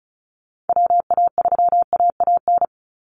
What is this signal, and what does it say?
Morse code, heard as a keyed tone, about 34 wpm, 700 Hz. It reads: WA3AAN
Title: Call Signs - 35wpm